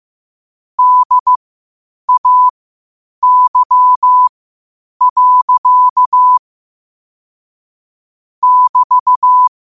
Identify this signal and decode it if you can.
Morse code, heard as a keyed tone, about 15 wpm, spaced as for 8 wpm, 1000 Hz. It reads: DAY. <BT>
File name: qotdfiles / 8wpm